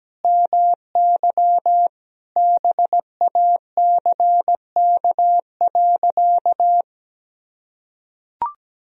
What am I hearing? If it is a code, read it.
Morse code, heard as a keyed tone, about 17 wpm, 700 Hz. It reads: MY BACK.
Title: Sentences from Top 1000 Words - Review Part 1 - 17wpm